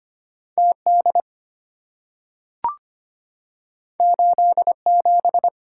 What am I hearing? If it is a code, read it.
Morse code, heard as a keyed tone, about 25 wpm, 700 Hz. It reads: TD 87
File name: Two Letter-Number Combinations - 25wpm